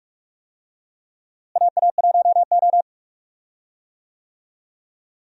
Morse code, heard as a keyed tone, about 46 wpm, 700 Hz. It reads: AA1O
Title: USA Call Signs - 4 Characters - Repeated 2x - 45wpm